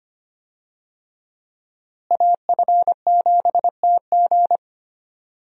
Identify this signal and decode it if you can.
Morse code, heard as a keyed tone, about 25 wpm, 700 Hz. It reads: AF7TG
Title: USA Call Signs - 5 Characters - Repeated 3x - 25wpm